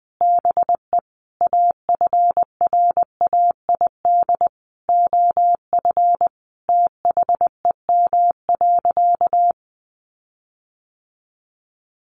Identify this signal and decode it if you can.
Morse code, heard as a keyed tone, about 20 wpm, 700 Hz. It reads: BE AFRAID OF THEM.
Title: Sentences from Top 1500 Words 20wpm